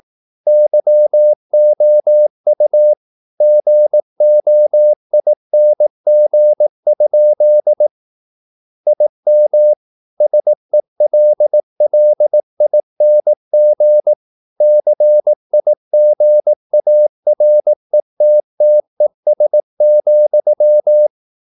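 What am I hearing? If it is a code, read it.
Morse code, heard as a keyed tone, about 18 wpm, 600 Hz. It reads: YOU GOING? IM SELLING CIGARETTES,
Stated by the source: Morse practice at 18 wpm, English text, from CW med Gnister.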